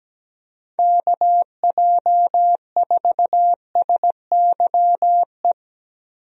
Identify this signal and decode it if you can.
Morse code, heard as a keyed tone, about 17 wpm, 700 Hz. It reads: KJ4SYE
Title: USA Call Signs - 6 Characters - Repeated 2x - 17wpm